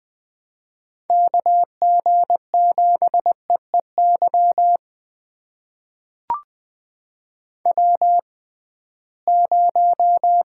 Morse code, heard as a keyed tone, about 20 wpm, spaced as for 15 wpm, 700 Hz. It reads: KG7EEY W0
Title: USA Call Signs - 6x Character Spacing - 20wpm